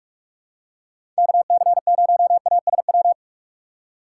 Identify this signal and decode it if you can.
Morse code, heard as a keyed tone, about 45 wpm, 700 Hz. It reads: KC0ASW
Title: USA Call Signs - 6 Characters - 8x Character Spacing - 45wpm